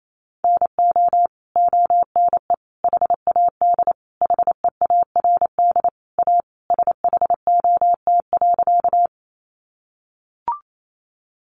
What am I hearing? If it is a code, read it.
Morse code, heard as a keyed tone, about 28 wpm, 700 Hz. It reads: NO ONE HAD HEARD A SHOT.
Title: Sentences from Top 1100 Words 28wpm